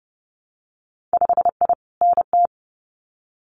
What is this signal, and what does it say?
Morse code, heard as a keyed tone, about 32 wpm, 700 Hz. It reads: 5I NT
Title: Winter Field Day Exchanges Encore 30wpm